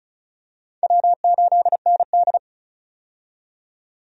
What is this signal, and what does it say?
Morse code, heard as a keyed tone, about 36 wpm, 700 Hz. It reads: W8ND
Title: USA Call Signs - 4 Characters - 35wpm